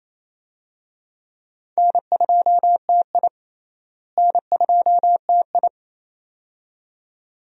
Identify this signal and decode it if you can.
Morse code, heard as a keyed tone, about 27 wpm, 700 Hz. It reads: N2TI N2TI
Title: USA Call Signs - 4 Characters - Repeated 2x - 28wpm